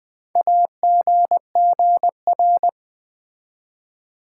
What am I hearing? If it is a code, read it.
Morse code, heard as a keyed tone, about 20 wpm, 700 Hz. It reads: AGGR
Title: Build A Word - 4 Letters At A Time - 10-Letter Words - 20wpm